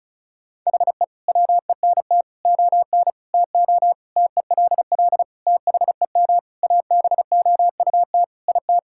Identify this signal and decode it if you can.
Morse code, heard as a keyed tone, about 36 wpm, 700 Hz. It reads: HE WENT ON TO TELL THEM ABOUT IT
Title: Sentences from Top 1000 Words - Review Part 6 - 35wpm